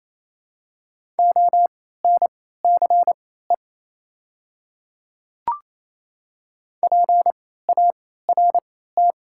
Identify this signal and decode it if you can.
Morse code, heard as a keyed tone, about 28 wpm, spaced as for 16 wpm, 700 Hz. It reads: ONCE PART
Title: 3x Character Spacing - Top 300 Words - 28wpm